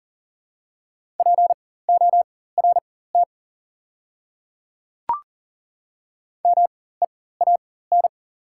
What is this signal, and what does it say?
Morse code, heard as a keyed tone, about 40 wpm, spaced as for 19 wpm, 700 Hz. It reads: PORT MEAN
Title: 4x Character Spacing - Top 200 Words - 40wpm